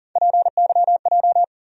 Morse code, heard as a keyed tone, about 40 wpm, 700 Hz. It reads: PYJ
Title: USA Call Sign Postfixes - 3 Letters 40wpm